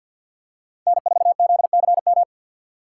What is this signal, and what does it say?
Morse code, heard as a keyed tone, about 52 wpm, 700 Hz. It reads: N4ZCM
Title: USA Call Signs - 4x Character Spacing - 50wpm